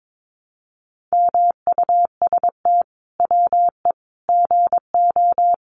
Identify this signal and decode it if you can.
Morse code, heard as a keyed tone, about 22 wpm, 700 Hz. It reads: MUST WE GO
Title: Sentences from Top 1400 Words 22wpm